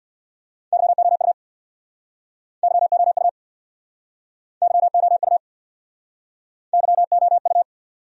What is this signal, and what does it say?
Morse code, heard as a keyed tone, about 67 wpm, 700 Hz. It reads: YOU YOU YOU YOU